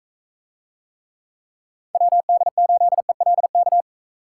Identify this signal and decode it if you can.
Morse code, heard as a keyed tone, about 42 wpm, 700 Hz. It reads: WD8ELK